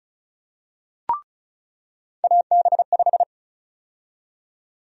Morse code, heard as a keyed tone, about 36 wpm, 700 Hz. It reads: AB5
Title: USA Call Sign Prefixes - 35wpm